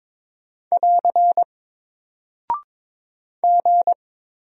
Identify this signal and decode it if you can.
Morse code, heard as a keyed tone, about 23 wpm, 700 Hz. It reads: <AR> G
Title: Single Character (Letter:Number:Punctuation:Prosign) Alt 22wpm